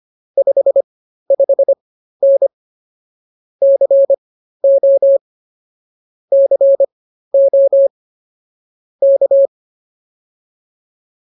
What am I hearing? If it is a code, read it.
Morse code, heard as a keyed tone, about 24 wpm, spaced as for 13 wpm, 550 Hz. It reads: 55N CO CO K